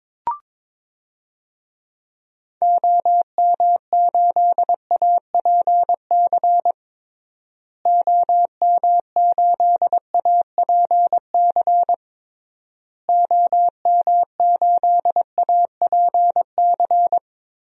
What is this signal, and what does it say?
Morse code, heard as a keyed tone, about 22 wpm, 700 Hz. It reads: OM8APC OM8APC OM8APC
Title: DXCC-100 Call Signs - 6 Characters - Repeated 3x - 22wpm